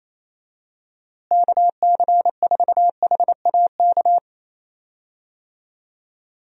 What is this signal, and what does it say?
Morse code, heard as a keyed tone, about 28 wpm, 700 Hz. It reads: KC4HAK
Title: USA Call Signs - 6 Characters - Repeated 2x - 28wpm